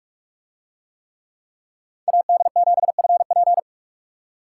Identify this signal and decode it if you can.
Morse code, heard as a keyed tone, about 45 wpm, 700 Hz. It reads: AD7FP